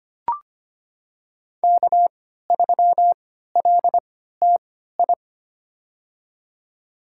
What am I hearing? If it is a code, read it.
Morse code, heard as a keyed tone, about 25 wpm, spaced as for 14 wpm, 700 Hz. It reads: K3LTI
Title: USA Call Signs - 3x Character Spacing - 25wpm